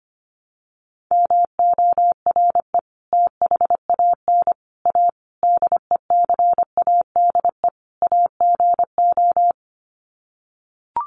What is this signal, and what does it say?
Morse code, heard as a keyed tone, about 25 wpm, 700 Hz. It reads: MORE THAN A DECADE AGO
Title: Sets of 5 Words Encore2 25wpm